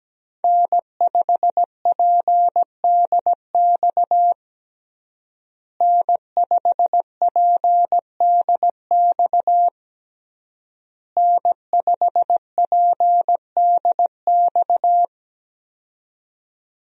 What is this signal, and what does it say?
Morse code, heard as a keyed tone, about 17 wpm, 700 Hz. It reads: N5PDX N5PDX N5PDX
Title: USA Call Signs - 5 Characters - Repeated 3x - 17wpm